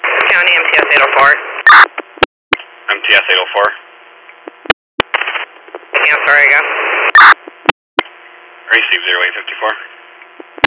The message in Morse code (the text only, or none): none